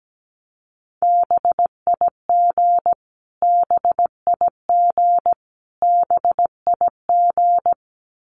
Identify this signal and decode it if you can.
Morse code, heard as a keyed tone, about 17 wpm, 700 Hz. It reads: BIG BIG BIG
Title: Top 200 Words Repeated 17wpm